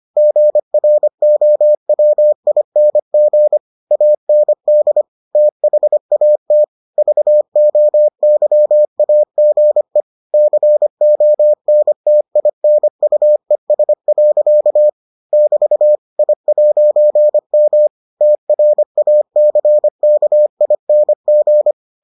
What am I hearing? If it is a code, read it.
Morse code, heard as a keyed tone, about 25 wpm, 600 Hz. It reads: GROWING AND THAT VOYAGE CONTINUES. <BT> I'M TRACKING